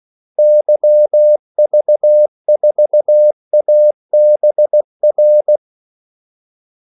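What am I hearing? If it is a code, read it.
Morse code, heard as a keyed tone, about 16 wpm, 600 Hz. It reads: YV4ABR
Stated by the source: Morse call sign practice at 16 wpm.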